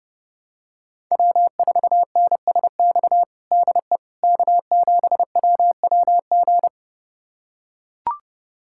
Morse code, heard as a keyed tone, about 32 wpm, 700 Hz. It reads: W4NSX DE K7WWG
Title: USA Callsign DE Callsign 30wpm